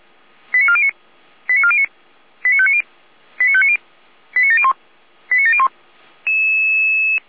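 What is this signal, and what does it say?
Five-tone selective call (ZVEI), heard as quick runs of five tones, tones of about 70 ms. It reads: 89409 89409 89590 89590 89701 89701
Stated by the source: ZVEI five-tone selective call 89126 received via SDR.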